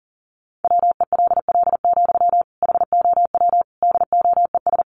Morse code, heard as a keyed tone, about 40 wpm, 700 Hz. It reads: WELL, HOW DOES
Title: Sentences from Top 1100 Words 40wpm